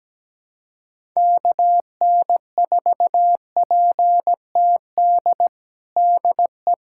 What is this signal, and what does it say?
Morse code, heard as a keyed tone, about 17 wpm, 700 Hz. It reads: KN4PTD DE
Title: USA Callsign DE Callsign - 6 Characters - 17wpm